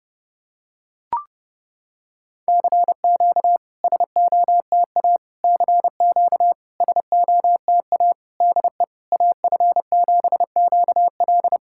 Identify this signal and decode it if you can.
Morse code, heard as a keyed tone, about 32 wpm, 700 Hz. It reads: CQ SOTA CQ SOTA DE AF7QL
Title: SOTA QSO Practice 30wpm